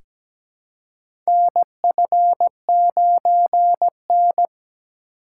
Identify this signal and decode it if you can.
Morse code, heard as a keyed tone, about 17 wpm, 700 Hz. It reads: NF9N
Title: USA Call Signs - 4 Characters - 2x Character Spacing - 17wpm